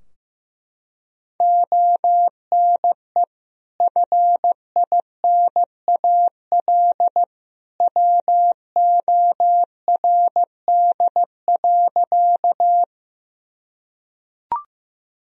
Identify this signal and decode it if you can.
Morse code, heard as a keyed tone, about 15 wpm, 700 Hz. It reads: ONE FINAL WORD.
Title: Sentences from Top 1000 Words - Review Part 2 - 15wpm